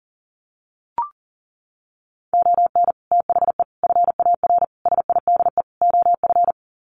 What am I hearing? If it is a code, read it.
Morse code, heard as a keyed tone, about 40 wpm, 700 Hz. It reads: ON THE FAR SIDE OF
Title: Sets of 5 Words 40wpm